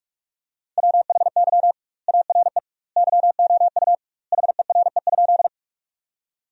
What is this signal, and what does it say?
Morse code, heard as a keyed tone, about 46 wpm, 700 Hz. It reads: WHY ARE YOU HERE?
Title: 2x Word-Spacing - Sentences from Top 200 Words - 45wpm